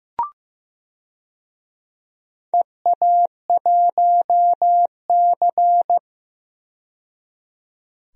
Morse code, heard as a keyed tone, about 15 wpm, 700 Hz. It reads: EA1C
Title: DXCC-100 Call Signs - 4 Characters - 15wpm